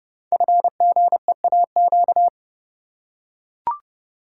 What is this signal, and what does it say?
Morse code, heard as a keyed tone, about 30 wpm, 700 Hz. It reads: FGEAQ
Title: Code Groups - 5 Letters - 30wpm